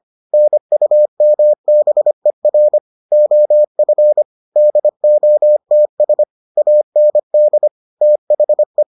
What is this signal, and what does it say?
Morse code, heard as a keyed tone, about 25 wpm, 600 Hz. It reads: NUMBER OF DOTS AND THE